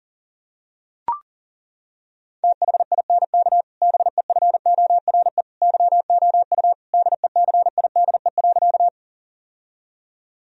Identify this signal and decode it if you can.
Morse code, heard as a keyed tone, about 37 wpm, 700 Hz. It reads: THINK BEFORE YOU DECIDE.
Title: Sentences from Top 1000 Words - Review Part 3 - 40wpm